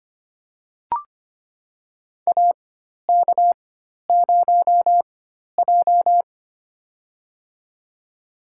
Morse code, heard as a keyed tone, about 25 wpm, spaced as for 12 wpm, 700 Hz. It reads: AK0J
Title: USA Call Signs - 4 Characters - 4x Character Spacing - 25wpm